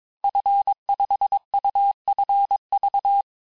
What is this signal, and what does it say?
Morse code, heard as a keyed tone, about 22 wpm, 800 Hz. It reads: F5UFV
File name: cours-du-mercredi / audios / cours-854-liste-fin-QSO-vitesse-21-mpm